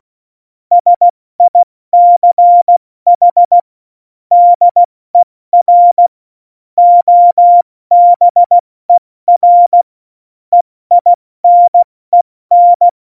Morse code, heard as a keyed tone, about 16 wpm, spaced as for 14 wpm, 700 Hz. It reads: SICH DER OBER EINEN
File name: Morsekurs / Witze / Witz 16. April 2019, 70 F80